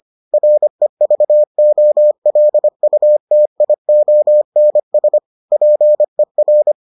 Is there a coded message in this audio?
Yes: Morse code from 0.3 to 6.7 s, heard as a keyed tone, about 25 wpm, 600 Hz. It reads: REVOLUTIONS PER